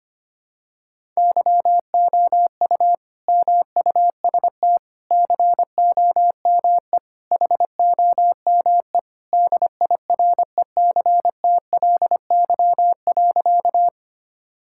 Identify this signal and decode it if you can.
Morse code, heard as a keyed tone, about 25 wpm, 700 Hz. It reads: YOU MUST COME HOME DIRECTLY.
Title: Sentences from Top 1000 Words - Review Part 5 - 25wpm